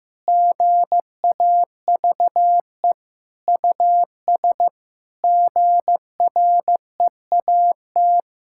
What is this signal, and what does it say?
Morse code, heard as a keyed tone, about 15 wpm, 700 Hz. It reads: GAVE US GREAT
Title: Sentences from Top 1000 Words - Review Part 5 - 15wpm